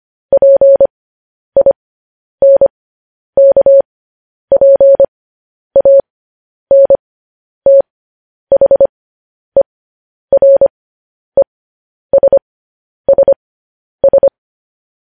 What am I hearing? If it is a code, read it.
Morse code, heard as a keyed tone, about 25 wpm, spaced as for 10 wpm, 550 Hz. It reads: PINKPANTHERESSS
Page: news-headlines-in-morse-code-at-10-wpm